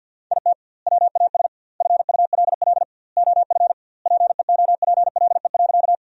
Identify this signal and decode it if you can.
Morse code, heard as a keyed tone, about 52 wpm, 700 Hz. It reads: IT WAS FULL OF PEOPLE.
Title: 2x Word-Spacing - Sentences from Top 500 Words - 50wpm